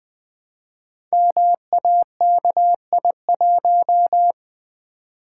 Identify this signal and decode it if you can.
Morse code, heard as a keyed tone, about 20 wpm, 700 Hz. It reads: MAKI1
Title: Code Groups - 5 Letter:Numbers - 20wpm